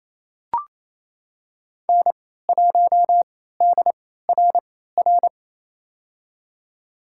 Morse code, heard as a keyed tone, about 28 wpm, spaced as for 16 wpm, 700 Hz. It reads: N1DRR
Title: USA Call Signs - 3x Character Spacing - 28wpm